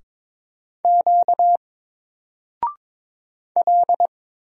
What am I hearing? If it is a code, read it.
Morse code, heard as a keyed tone, about 22 wpm, 700 Hz. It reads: Q L
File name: Single-Letter-Number-22wpm